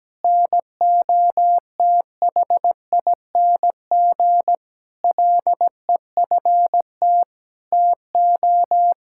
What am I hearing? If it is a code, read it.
Morse code, heard as a keyed tone, about 17 wpm, 700 Hz. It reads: NOTHING LEFT TO
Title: Sentences from Top 1000 Words - Review Part 6 - 17wpm